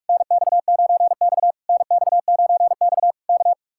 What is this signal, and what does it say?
Morse code, heard as a keyed tone, about 56 wpm, 700 Hz. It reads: NX9X NX9X K
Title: Speed-Racing Common QSO Phrases - 45wpm